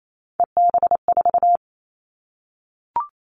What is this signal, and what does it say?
Morse code, heard as a keyed tone, about 28 wpm, 700 Hz. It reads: EB4
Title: Callsign Prefixes for Top 25 DXCC - Repeated 3x - 28wpm